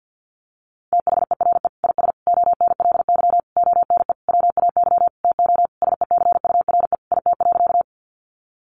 Morse code, heard as a keyed tone, about 50 wpm, 700 Hz. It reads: THERE IS ONLY ONE WAY TO SECURE IT.
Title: Sentences from Top 1200 Words 50wpm